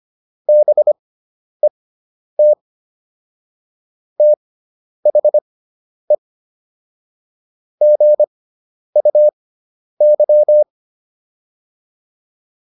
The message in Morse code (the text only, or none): BET THE GUY